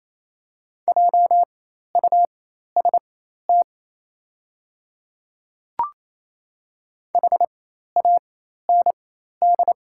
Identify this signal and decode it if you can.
Morse code, heard as a keyed tone, about 27 wpm, spaced as for 13 wpm, 690 Hz. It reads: JUST HAND
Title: 4x Character Spacing - Top 200 Words - 28wpm